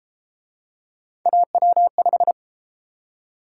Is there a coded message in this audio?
Yes: Morse code from 1.3 to 2.3 s, heard as a keyed tone, about 30 wpm, 700 Hz. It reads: AW5